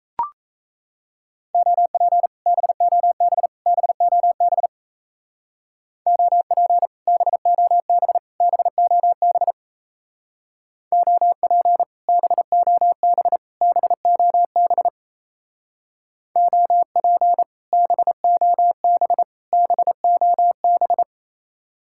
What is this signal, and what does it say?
Morse code, heard as a keyed tone, about 37 wpm, 700 Hz. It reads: OP BOB BOB OP BOB BOB OP BOB BOB OP BOB BOB